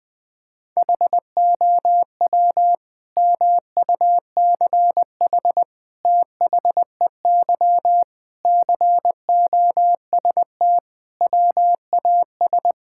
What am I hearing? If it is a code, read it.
Morse code, heard as a keyed tone, about 20 wpm, 700 Hz. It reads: HOW MUCH THEY COST WAS
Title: Sentences from Top 1000 Words - Review Part 5 - 20wpm